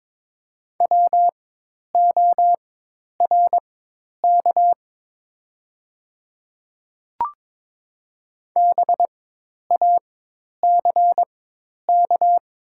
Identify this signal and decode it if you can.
Morse code, heard as a keyed tone, about 22 wpm, spaced as for 10 wpm, 700 Hz. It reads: WORK BACK
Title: 4x Character Spacing - Top 200 Words - 22wpm